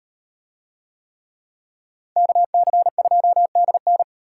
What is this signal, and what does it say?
Morse code, heard as a keyed tone, about 38 wpm, 700 Hz. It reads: KC2DN